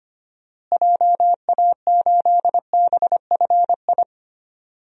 Morse code, heard as a keyed tone, about 25 wpm, 700 Hz. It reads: JA8BFI
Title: DXCC-100 Call Signs - 6 Characters - Repeated 3x - 25wpm